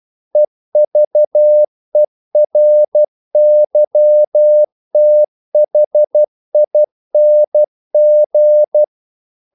Morse code, heard as a keyed tone, about 12 wpm, 600 Hz. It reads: EVERYTHING
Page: English quotations in CW (telegraphy - morse)